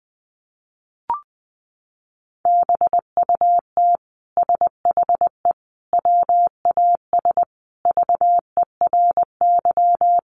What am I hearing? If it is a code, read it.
Morse code, heard as a keyed tone, about 20 wpm, 700 Hz. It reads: BUT SHE WAS VERY
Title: Sentences from Top 1000 Words 20wpm